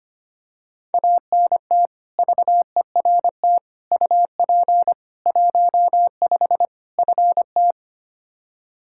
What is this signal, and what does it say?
Morse code, heard as a keyed tone, about 25 wpm, 700 Hz. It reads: ANT VERT UP 15 FT
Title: K1USN SST QSO Practice 25wpm